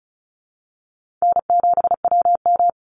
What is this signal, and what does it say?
Morse code, heard as a keyed tone, about 34 wpm, 700 Hz. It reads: N7WM
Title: NA SOTA Chaser Callsigns 35wpm